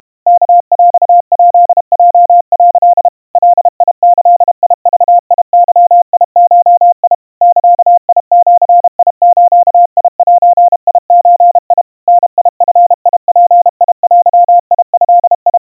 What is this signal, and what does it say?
Morse code (Japanese Wabun code), heard as a keyed tone, about 32 wpm, 700 Hz. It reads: ワヰヱヲ」 ガギグゲゴ ザジズゼゾ ダヂヅデド
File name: 110 cpm 700 Hz Crispy